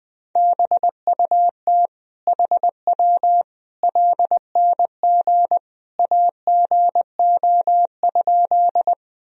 Morse code, heard as a keyed tone, about 20 wpm, 700 Hz. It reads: BUT HW LNG AGO?
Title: Sentences from Top 500 Words and Abbreviations 20wpm